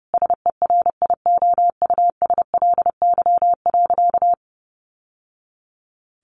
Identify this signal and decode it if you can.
Morse code, heard as a keyed tone, about 30 wpm, 700 Hz. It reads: SERIOUSLY.
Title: Sentences from Top 1200 Words 30wpm